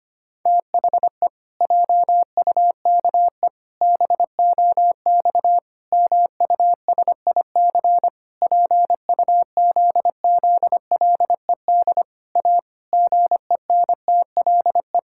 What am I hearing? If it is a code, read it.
Morse code, heard as a keyed tone, about 25 wpm, 700 Hz. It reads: THE JUKE BOX MUSIC PUZZLED A GENTLE
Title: Pangram Challenge v2 - 25wpm